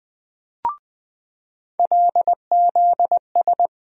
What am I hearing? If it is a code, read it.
Morse code, heard as a keyed tone, about 20 wpm, 700 Hz. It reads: LZS
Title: USA Call Sign Postfixes - 3 Letters 20wpm